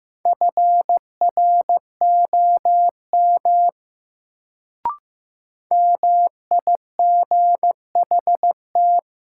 Morse code, heard as a keyed tone, about 15 wpm, 700 Hz. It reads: FROM MIGHT
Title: Sets of 2 Words - New Word Drill - 15wpm